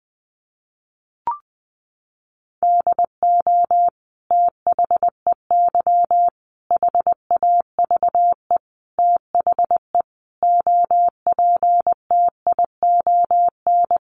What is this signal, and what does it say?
Morse code, heard as a keyed tone, about 20 wpm, 700 Hz. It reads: DO THEY HAVE THE OPTION
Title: Sentences from Top 1300 Words 20wpm